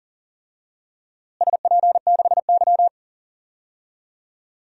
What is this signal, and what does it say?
Morse code, heard as a keyed tone, about 40 wpm, 700 Hz. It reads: SP6Y
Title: DXCC-100 Call Signs - 4 Characters - Repeated 3x - 40wpm